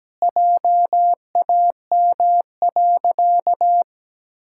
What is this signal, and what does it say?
Morse code, heard as a keyed tone, about 17 wpm, 700 Hz. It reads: JAM.
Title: Pangram Challenge v2 - 17wpm